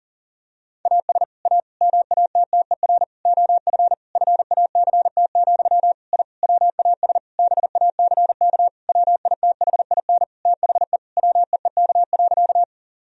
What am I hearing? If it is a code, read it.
Morse code, heard as a keyed tone, about 40 wpm, 700 Hz. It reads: AS A MATTER OF FACT, I WAS BACK WITHIN THE WEEK.
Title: Sentences from Top 1000 Words - Review Part 4 - 40wpm